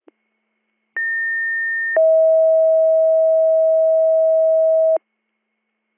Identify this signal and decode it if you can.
Two-tone page, heard as a one-second tone, then a three-second tone, about 1800 then 640 Hz. It is heard from 1.0 to 5.0 s.